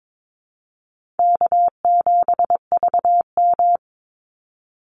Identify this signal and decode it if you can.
Morse code, heard as a keyed tone, about 22 wpm, 690 Hz. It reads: K7VM
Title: Call Signs Repeated 2x - 22wpm